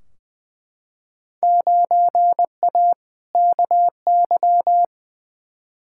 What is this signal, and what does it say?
Morse code, heard as a keyed tone, about 20 wpm, 700 Hz. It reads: 9A KY
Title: Field Day Exchanges Encore - 20wpm